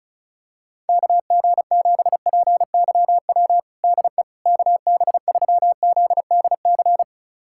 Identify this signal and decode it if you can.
Morse code, heard as a keyed tone, about 36 wpm, 700 Hz. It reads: KG7PYW DE KB3ZDC
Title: USA Callsign DE Callsign - 6 Characters - 35wpm